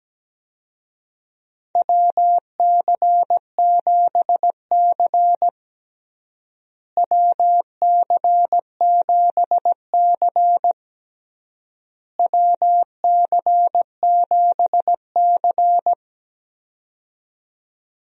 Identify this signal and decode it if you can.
Morse code, heard as a keyed tone, about 17 wpm, 700 Hz. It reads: WC7C WC7C WC7C
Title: USA Call Signs - 4 Characters - Repeated 3x - 17wpm